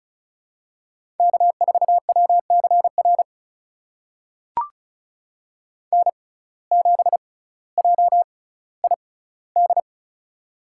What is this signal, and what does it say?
Morse code, heard as a keyed tone, about 36 wpm, spaced as for 18 wpm, 700 Hz. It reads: K4WCR N7JID
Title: USA Call Signs - 6x Character Spacing - 35wpm